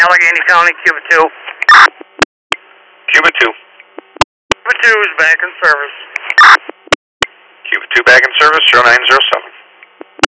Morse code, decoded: none

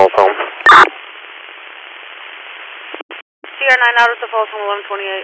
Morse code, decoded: none

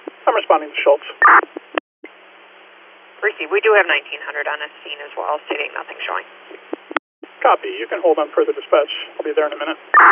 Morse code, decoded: none

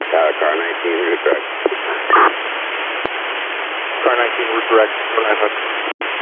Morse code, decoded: none

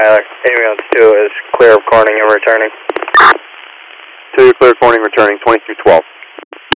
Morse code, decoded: none

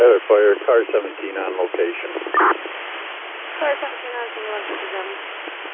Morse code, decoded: none

